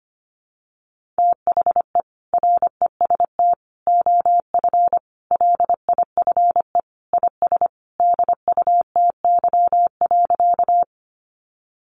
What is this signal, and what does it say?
Morse code, heard as a keyed tone, about 25 wpm, 700 Hz. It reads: THE REST OF LIFE IS DUTY.
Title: Sentences from Top 1500 Words 25wpm